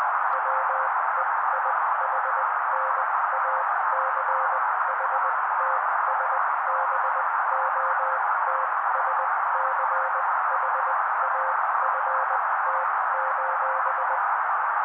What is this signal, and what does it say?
Morse code, heard as a keyed tone, about 20 wpm, spaced as for 16 wpm, 530 Hz. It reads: WEIHNACHTSBOTSCHAFT:
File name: daily / 16wpm noise